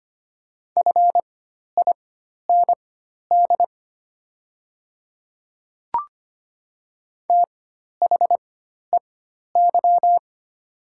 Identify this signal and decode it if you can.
Morse code, heard as a keyed tone, about 25 wpm, spaced as for 12 wpm, 700 Hz. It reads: FIND THEY
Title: 4x Character Spacing - Top 300 Words - 25wpm